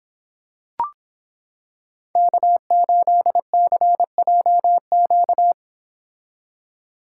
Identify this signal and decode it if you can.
Morse code, heard as a keyed tone, about 26 wpm, 700 Hz. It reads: K8CJQ